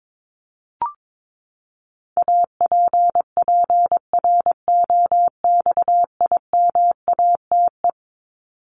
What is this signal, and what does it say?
Morse code, heard as a keyed tone, about 22 wpm, 700 Hz. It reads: APPROXIMATE
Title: LY Suffix - Top 100 Words with Base Words 22wpm